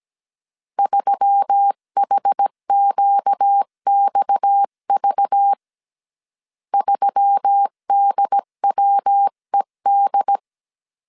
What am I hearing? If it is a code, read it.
Morse code, heard as a keyed tone, about 17 wpm, 800 Hz. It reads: 3HQXV 3DWED
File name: CW / DET / 1 6 1500C S C 16wpm 2733